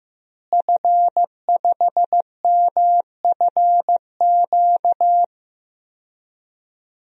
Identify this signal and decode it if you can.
Morse code, heard as a keyed tone, about 15 wpm, 700 Hz. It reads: F5MFQ
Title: DXCC-100 Call Signs - 5 Characters - Repeated 2x - 15wpm